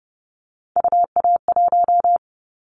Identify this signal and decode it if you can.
Morse code, heard as a keyed tone, about 31 wpm, 700 Hz. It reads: UA1
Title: Callsign Prefixes for Top 50 DXCC - 30wpm